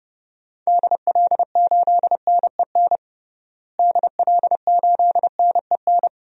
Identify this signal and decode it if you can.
Morse code, heard as a keyed tone, about 30 wpm, 700 Hz. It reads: DL8NEN DL8NEN
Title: DXCC-100 Call Signs - 6 Characters - Repeated 3x - 30wpm